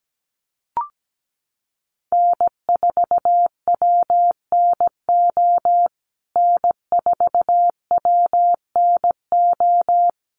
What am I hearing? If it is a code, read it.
Morse code, heard as a keyed tone, about 17 wpm, 700 Hz. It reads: N4WNO N4WNO
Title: USA Call Signs Repeated 2x 17wpm